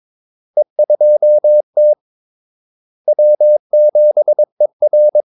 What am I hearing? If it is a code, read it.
Morse code, heard as a keyed tone, about 22 wpm, 600 Hz. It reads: E2T W7ER
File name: Callsigns-22wpm